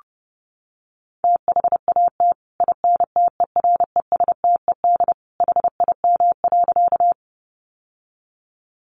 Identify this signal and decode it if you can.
Morse code, heard as a keyed tone, about 30 wpm, 700 Hz. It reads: THAT INTERESTED HIM.
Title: Sentences from Top 900 Words 30wpm